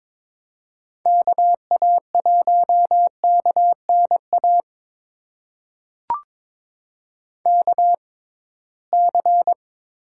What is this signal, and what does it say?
Morse code, heard as a keyed tone, about 21 wpm, spaced as for 16 wpm, 700 Hz. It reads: KA1KNA KC
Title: USA Call Signs - 6 Characters - 6x Character Spacing - 22wpm